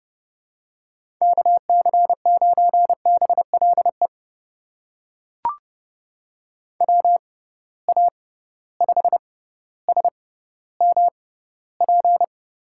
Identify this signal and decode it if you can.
Morse code, heard as a keyed tone, about 30 wpm, spaced as for 15 wpm, 700 Hz. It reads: KC9BLE WA5SMP
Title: USA Call Signs - 6 Characters - 6x Character Spacing - 30wpm